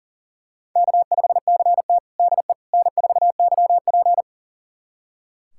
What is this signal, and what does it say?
Morse code, heard as a keyed tone, about 40 wpm, 700 Hz. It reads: K5CT DE N4YP
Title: USA Callsign DE Callsign - 4 Characters - 40wpm